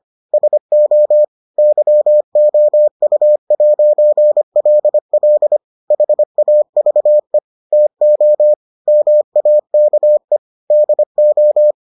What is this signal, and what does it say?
Morse code, heard as a keyed tone, about 25 wpm, 600 Hz. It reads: SO YOU'LL HAVE TO MAKE DO